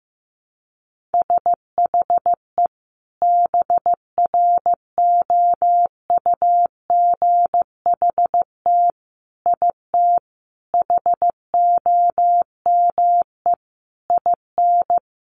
Morse code, heard as a keyed tone, about 15 wpm, 700 Hz. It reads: SHE BROUGHT IT HOME IN
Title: Sentences from Top 500 Words 15wpm